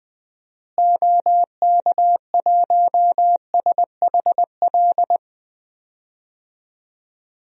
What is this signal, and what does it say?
Morse code, heard as a keyed tone, about 21 wpm, 700 Hz. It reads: OK1SHL
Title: DXCC-100 Call Signs - 6 Characters - Repeated 3x - 20wpm